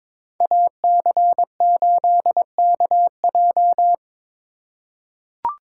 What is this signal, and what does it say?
Morse code, heard as a keyed tone, about 22 wpm, 700 Hz. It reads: AC8KJ
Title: USA Call Signs - 2x Character Spacing - 22wpm